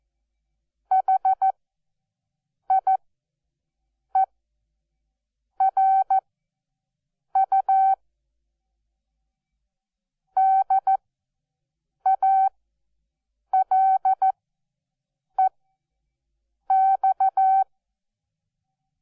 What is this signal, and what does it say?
Morse code, heard as a keyed tone, about 14 wpm, spaced as for 6 wpm, 770 Hz. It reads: HIERU DALEX
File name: Onlinelehrgang / morsen / DJ4UF Morsen12